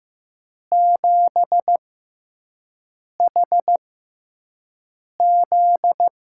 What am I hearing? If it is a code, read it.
Morse code, heard as a keyed tone, about 15 wpm, spaced as for 5 wpm, 700 Hz. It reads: 7HZ
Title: USA Call Signs - 5 Characters - 6x Character Spacing - 15wpm